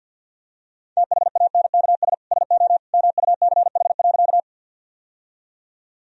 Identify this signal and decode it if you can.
Morse code, heard as a keyed tone, about 52 wpm, 700 Hz. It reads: THANKS SO MUCH.
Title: Sentences from Top 1000 Words - Review Part 3 - 50wpm